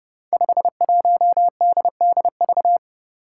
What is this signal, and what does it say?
Morse code, heard as a keyed tone, about 32 wpm, 700 Hz. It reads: 51DDV
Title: Five Challenging Letter-Number Combinations 30wpm